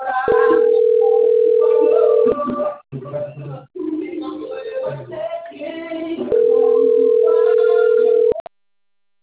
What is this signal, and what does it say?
Ringback tone (North America), 440 + 480 Hz.